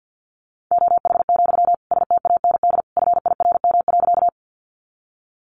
Morse code, heard as a keyed tone, about 50 wpm, 700 Hz. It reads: OH, STAND FIRM.
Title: Sentences from Top 1400 Words 50wpm